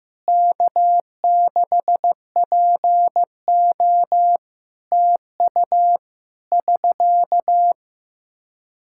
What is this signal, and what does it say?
Morse code, heard as a keyed tone, about 15 wpm, 700 Hz. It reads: K6PO TU <SK>
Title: Practice QSO - Exchange 4 - Ending - 15wpm